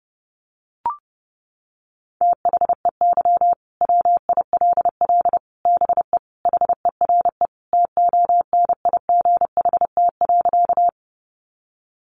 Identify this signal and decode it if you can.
Morse code, heard as a keyed tone, about 30 wpm, 700 Hz. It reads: THEY WILL BE HERE TONIGHT.
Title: Sentences from Top 600 Words 30wpm